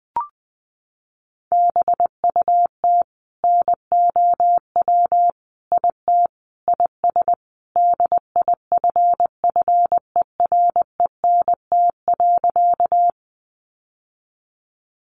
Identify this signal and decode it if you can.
Morse code, heard as a keyed tone, about 20 wpm, 700 Hz. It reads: BUT NOW IT IS DIFFERENT.
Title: Sentences from Top 600 Words 20wpm